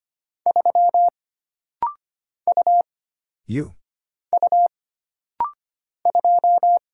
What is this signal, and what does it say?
Morse code, heard as a keyed tone, about 25 wpm, 700 Hz. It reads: 3 U U 2
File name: Single Character (Letter:Number:Punctuation:Prosign) - Rapid-Fire - 25wpm